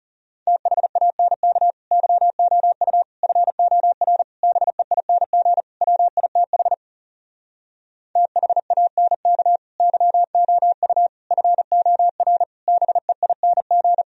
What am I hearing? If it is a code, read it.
Morse code, heard as a keyed tone, about 40 wpm, 700 Hz. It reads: THANK YOU FOR BEING WITH THANK YOU FOR BEING